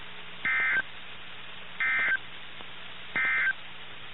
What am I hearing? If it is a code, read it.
Emergency Alert System (SAME) tones, heard as harsh buzzing bursts. It is heard from 0.3 to 3.6 s.
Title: NOAA Weather Radio - KIH61